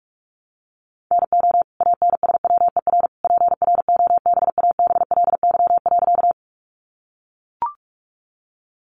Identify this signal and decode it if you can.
Morse code, heard as a keyed tone, about 44 wpm, 700 Hz. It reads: NO ANSWER PROBABLY.
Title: Sentences from Top 700 Words 45wpm